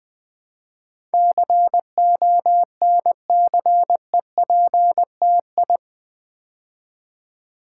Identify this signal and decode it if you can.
Morse code, heard as a keyed tone, about 20 wpm, 700 Hz. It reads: CONCEPTI
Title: Build A Word - 4 Letters At A Time - 10-Letter Words - 20wpm